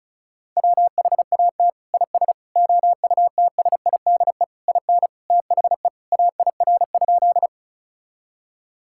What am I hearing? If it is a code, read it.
Morse code, heard as a keyed tone, about 36 wpm, 700 Hz. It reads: WHAT IS OUTSIDE IN THE AIR?
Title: Sentences from Top 1000 Words - Review Part 6 - 35wpm